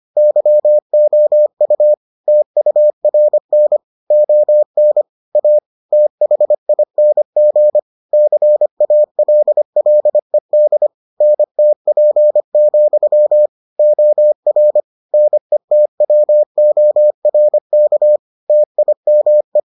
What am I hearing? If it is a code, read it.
Morse code, heard as a keyed tone, about 25 wpm, 600 Hz. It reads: YOU TURN ON A THING CALLED NTP, OR NETWORK TIME